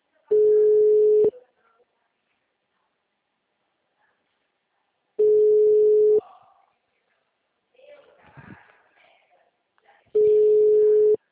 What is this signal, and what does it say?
Ringback tone (Europe, 425 Hz), with speech.
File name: out- -2004- - - .470